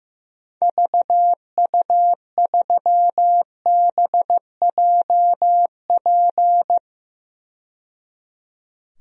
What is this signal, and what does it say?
Morse code, heard as a keyed tone, about 15 wpm, 700 Hz. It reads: VU3BJP
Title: DXCC-100 Call Signs - 6 Characters - 15wpm